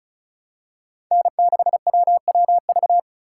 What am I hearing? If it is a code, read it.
Morse code, heard as a keyed tone, about 34 wpm, 700 Hz. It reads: N6WWV
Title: USA Call Signs - 5 Characters - 3x Character Spacing - 35wpm